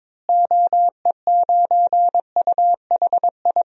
Morse code, heard as a keyed tone, about 22 wpm, 700 Hz. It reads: OE9UHI
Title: DXCC-100 Call Signs - 6 Characters - Repeated 2x - 22wpm